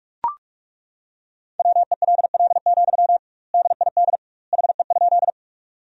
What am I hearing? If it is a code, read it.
Morse code, heard as a keyed tone, about 41 wpm, 700 Hz. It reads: WELL, DID HE?
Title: 2x Word-Spacing - Sentences from Top 200 Words - 45wpm